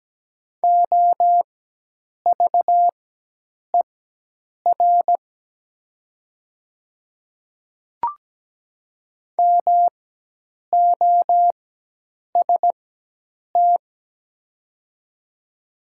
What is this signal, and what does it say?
Morse code, heard as a keyed tone, about 17 wpm, spaced as for 8 wpm, 700 Hz. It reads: OVER MOST
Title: 4x Character Spacing - Top 100 Words - 17wpm